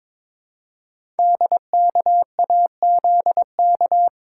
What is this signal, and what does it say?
Morse code, heard as a keyed tone, about 22 wpm, 700 Hz. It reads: DKAZK
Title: Code Groups - 5 Letters - 22wpm